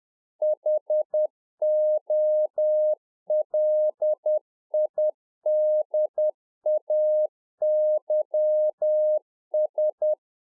Morse code, heard as a keyed tone, about 10 wpm, 610 Hz. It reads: HOLIDAYS
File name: 2011-04-mp3 / A SMALL TRANCEIVER 1 10WPM